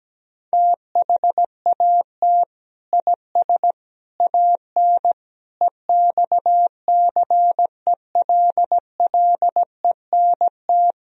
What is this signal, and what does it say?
Morse code, heard as a keyed tone, about 17 wpm, 700 Hz. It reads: THAT IS AN EXCELLENT
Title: Sentences from Top 1000 Words - Review Part 1 - 17wpm